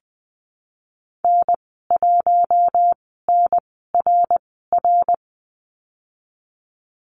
Morse code, heard as a keyed tone, about 20 wpm, spaced as for 15 wpm, 700 Hz. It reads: N1NRR
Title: 055 Lesson 10g - Letter r - Call Signs 20wpm